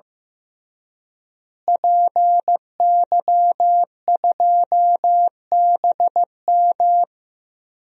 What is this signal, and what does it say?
Morse code, heard as a keyed tone, about 15 wpm, 700 Hz. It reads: PY2BM
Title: DXCC-100 Call Signs - 5 Characters - Repeated 3x - 15wpm